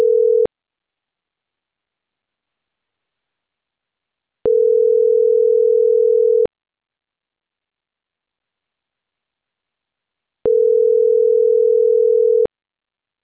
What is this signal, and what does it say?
Ringback tone (North America), 440 + 480 Hz.